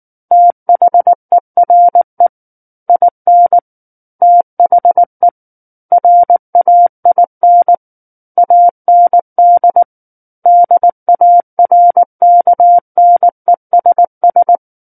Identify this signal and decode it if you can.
Morse code, heard as a keyed tone, about 19 wpm, 700 Hz. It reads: THERE IN THE RAIN AND DARKNESS
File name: Book One / 10-Chapter 10 - 19 WPM